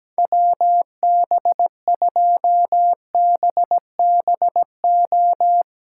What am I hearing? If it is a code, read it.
Morse code, heard as a keyed tone, about 17 wpm, 700 Hz. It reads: WB2BBO
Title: Speed Racing - USA 2x3 Call Signs 17wpm